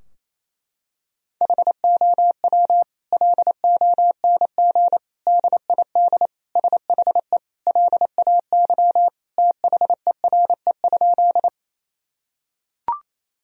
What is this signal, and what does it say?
Morse code, heard as a keyed tone, about 28 wpm, 700 Hz. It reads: HOW LONG DID SHE LAY THERE?
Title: Sentences from Top 1000 Words - Review Part 2 - 28wpm